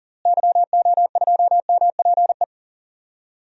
Morse code, heard as a keyed tone, about 40 wpm, 700 Hz. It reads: YO2MPE
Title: DXCC-100 Call Signs - 6 Characters - Repeated 2x - 40wpm